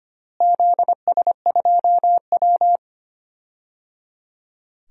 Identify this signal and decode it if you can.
Morse code, heard as a keyed tone, about 25 wpm, 700 Hz. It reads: ZS2W
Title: DXCC-100 Call Signs - 4 Characters - Repeated 3x - 25wpm